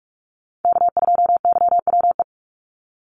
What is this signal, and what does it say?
Morse code, heard as a keyed tone, about 44 wpm, 700 Hz. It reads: K2YWE
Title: Call Signs - 45wpm